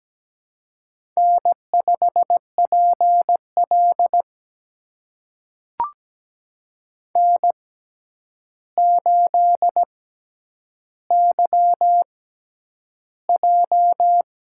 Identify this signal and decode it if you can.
Morse code, heard as a keyed tone, about 17 wpm, spaced as for 9 wpm, 700 Hz. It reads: N5PL N8YJ
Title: USA Call Signs - 4 Characters - 6x Character Spacing - 17wpm